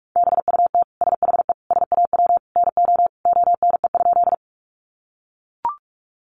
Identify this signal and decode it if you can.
Morse code, heard as a keyed tone, about 44 wpm, 700 Hz. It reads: BUT SHE SAW NO ONE?
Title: Sentences from Top 300 Words 45wpm